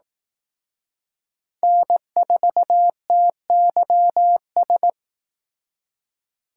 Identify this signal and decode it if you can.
Morse code, heard as a keyed tone, about 18 wpm, 700 Hz. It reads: N4TYS
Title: Speed Racing - USA 1x3 Call Signs 15wpm